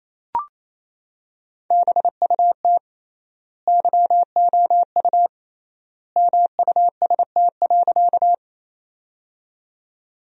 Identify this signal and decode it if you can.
Morse code, heard as a keyed tone, about 28 wpm, 700 Hz. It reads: BUT YOU MUST.
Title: 3x Word-Spacing - Sentences from Top 200 Words - 28wpm